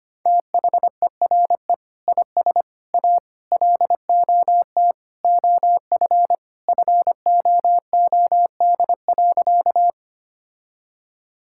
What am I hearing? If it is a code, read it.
Morse code, heard as a keyed tone, about 25 wpm, 700 Hz. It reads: THERE IS A LOT OF FOOD.
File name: practice / 25wpm / Sentences with 3-Word Phrases - 25wpm